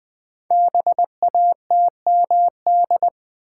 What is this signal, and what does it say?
Morse code, heard as a keyed tone, about 20 wpm, 700 Hz. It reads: BATMD
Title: Code Groups - 5 Letters - 20wpm